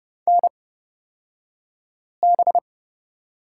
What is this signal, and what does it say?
Morse code, heard as a keyed tone, about 29 wpm, spaced as for 5 wpm, 700 Hz. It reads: NB
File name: Lesson 37d - Letter x - Two Characters 30wpm